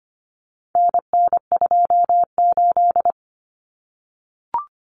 Morse code, heard as a keyed tone, about 25 wpm, 700 Hz. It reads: NN28
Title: Common Cut Numbers - 4-Digit 25wpm